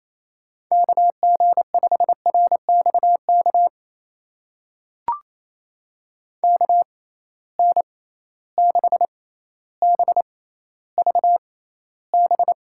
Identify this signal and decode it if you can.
Morse code, heard as a keyed tone, about 28 wpm, spaced as for 14 wpm, 700 Hz. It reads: KG5RXK KN6BVB
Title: USA Call Signs - 6x Character Spacing - 28wpm